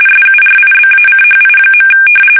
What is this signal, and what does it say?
RTTY, about 45.45 baud. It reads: SURIBACHI.
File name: fsk850 / battleshipnj